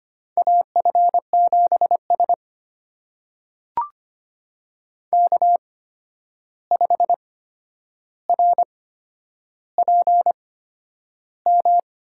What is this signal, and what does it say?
Morse code, heard as a keyed tone, about 25 wpm, spaced as for 10 wpm, 700 Hz. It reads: AF7S K5RPM
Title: USA Call Signs - 8x Character Spacing - 25wpm